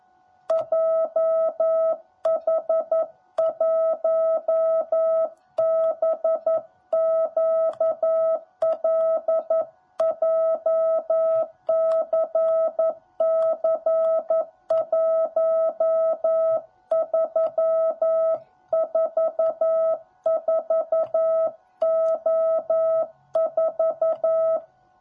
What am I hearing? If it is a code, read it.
Morse code, heard as a keyed tone, about 11 wpm, 650 Hz. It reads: JH1BQL JCC1344O4
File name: jh8cbh / 2021cw / cw10b